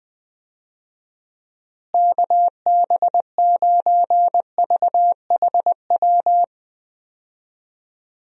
Morse code, heard as a keyed tone, about 21 wpm, 700 Hz. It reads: KB9VHW
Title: USA Call Signs - 6 Characters - Repeated 2x - 20wpm